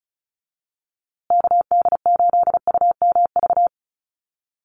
Morse code, heard as a keyed tone, about 34 wpm, 700 Hz. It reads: KD8UMV
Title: USA Call Signs Repeated 2x 35wpm